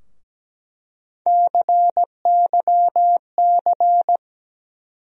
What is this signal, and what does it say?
Morse code, heard as a keyed tone, about 17 wpm, 700 Hz. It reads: CYC
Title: USA Call Sign Postfixes - 3 Letters 17wpm